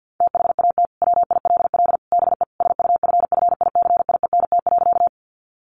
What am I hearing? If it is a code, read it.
Morse code, heard as a keyed tone, about 50 wpm, 700 Hz. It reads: THAT WILL BE SUFFICIENT.
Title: Sentences from Top 1400 Words 50wpm